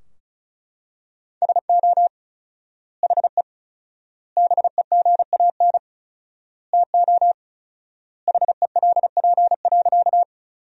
Morse code, heard as a keyed tone, about 36 wpm, 700 Hz. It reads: SO HE BEGAN TO HELP.
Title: 4x Word-Spacing - Sentences from Top 300 Words - 35wpm